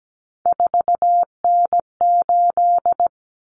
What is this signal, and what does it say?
Morse code, heard as a keyed tone, about 17 wpm, 690 Hz. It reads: 4N8
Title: Common Cut Numbers - 3-Digit 17wpm